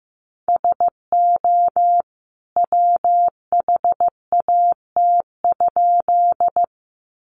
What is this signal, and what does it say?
Morse code, heard as a keyed tone, about 15 wpm, 700 Hz. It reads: SO WHAT?
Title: A Sound Of Thunder 15wpm